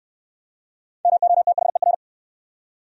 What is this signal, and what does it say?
Morse code, heard as a keyed tone, about 62 wpm, 700 Hz. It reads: MOTHER